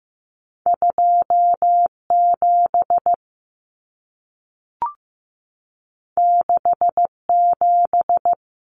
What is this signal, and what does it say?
Morse code, heard as a keyed tone, about 15 wpm, 700 Hz. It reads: 27 67
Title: Common Cut Numbers - 2-Digit 15wpm